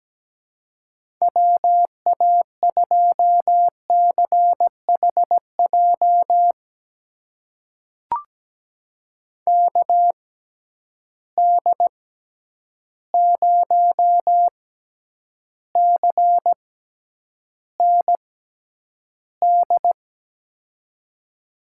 Morse code, heard as a keyed tone, about 17 wpm, spaced as for 9 wpm, 700 Hz. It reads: WA2CHJ KD0CND
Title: USA Call Signs - 6 Characters - 6x Character Spacing - 17wpm